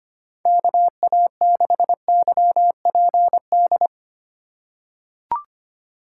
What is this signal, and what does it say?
Morse code, heard as a keyed tone, about 25 wpm, 700 Hz. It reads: KA6YPD
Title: USA Call Signs - 6 Characters - 4x Character Spacing - 25wpm